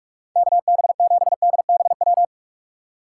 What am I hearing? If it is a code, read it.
Morse code, heard as a keyed tone, about 44 wpm, 700 Hz. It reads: KB7DBW
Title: USA Call Signs - 6 Characters - 2x Character Spacing - 45wpm